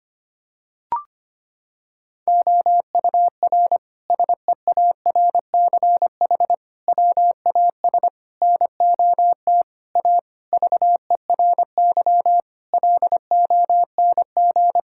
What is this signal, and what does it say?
Morse code, heard as a keyed tone, about 25 wpm, 700 Hz. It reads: OUR SEARCH WAS NOT A VERY LONG
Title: Sentences from Top 1000 Words - Review Part 5 - 25wpm